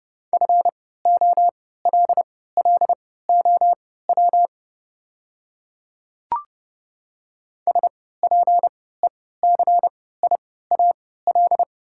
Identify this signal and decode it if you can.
Morse code, heard as a keyed tone, about 30 wpm, spaced as for 17 wpm, 690 Hz. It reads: FOLLOW SPECIAL
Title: 3x Character Spacing - Top 500 Words - 30wpm